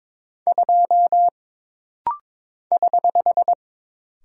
Morse code, heard as a keyed tone, about 22 wpm, 700 Hz. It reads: 2 <HH>
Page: This rapid-fire timing is geared to push your ICR (Instant Character Recognition) skills.